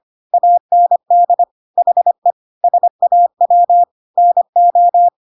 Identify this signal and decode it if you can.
Morse code, heard as a keyed tone, about 25 wpm, 700 Hz. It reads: AND HE SAW NO
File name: Pinocchio / Pinocchio ch1 25 25 ik1hge